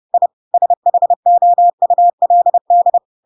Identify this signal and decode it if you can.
Morse code, heard as a keyed tone, about 30 wpm, 700 Hz. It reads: I SHOULD
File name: Pinocchio / Pinocchio ch3 30 30 ik1hge